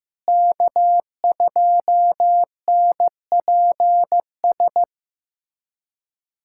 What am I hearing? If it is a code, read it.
Morse code, heard as a keyed tone, about 15 wpm, 700 Hz. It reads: K2NPS
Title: USA Callsign DE Callsign 15wpm